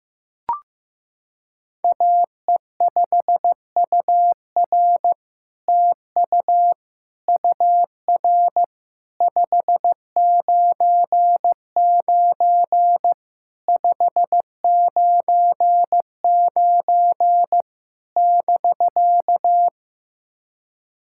Morse code, heard as a keyed tone, about 15 wpm, 700 Hz. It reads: AE5UR TU UR 599 599 <BK>
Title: SOTA QSO Practice 15wpm